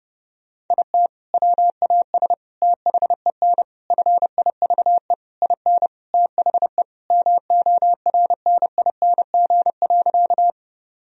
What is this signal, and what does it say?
Morse code, heard as a keyed tone, about 30 wpm, 700 Hz. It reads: IT WAS THEN FIVE IN THE MORNING.
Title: Sentences from Top 1000 Words - Review Part 2 - 30wpm